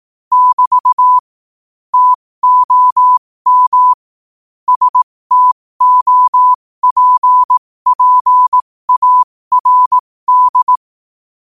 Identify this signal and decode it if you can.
Morse code, heard as a keyed tone, about 18 wpm, spaced as for 15 wpm, 1000 Hz. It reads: <BT> TOM STOPPARD
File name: 15wpm